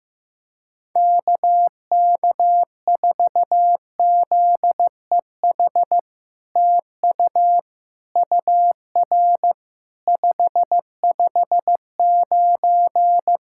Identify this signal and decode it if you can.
Morse code, heard as a keyed tone, about 15 wpm, 700 Hz. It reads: KK4ZEH TU UR 559
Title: SOTA QSO Practice 15wpm